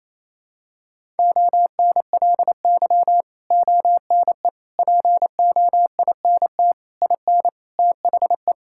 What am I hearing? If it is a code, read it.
Morse code, heard as a keyed tone, about 28 wpm, 700 Hz. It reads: ONLY ONE POINT IN THE
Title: Sentences from Top 1000 Words - Review Part 5 - 28wpm